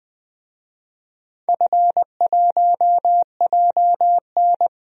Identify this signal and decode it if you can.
Morse code, heard as a keyed tone, about 20 wpm, 700 Hz. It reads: F1JN
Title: DXCC-100 Call Signs - 4 Characters - Repeated 2x - 20wpm